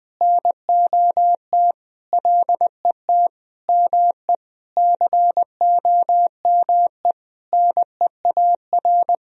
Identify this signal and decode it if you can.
Morse code, heard as a keyed tone, about 20 wpm, 700 Hz. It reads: NOT LET ME COME NEAR
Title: Sentences from Top 1000 Words - Review Part 1 - 20wpm